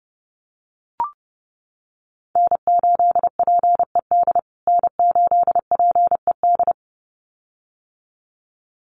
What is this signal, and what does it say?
Morse code, heard as a keyed tone, about 30 wpm, 700 Hz. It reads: N8PED N8PED
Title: USA Call Signs Repeated 2x 30wpm